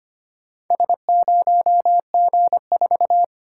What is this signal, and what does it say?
Morse code, heard as a keyed tone, about 25 wpm, 700 Hz. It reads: S0G4
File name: practice / 25wpm / Four Letter-Number Combinations - 25wpm